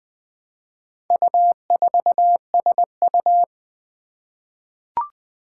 Morse code, heard as a keyed tone, about 20 wpm, 700 Hz. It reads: U4SU
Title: Four Challenging Letter-Number Combinations 20wpm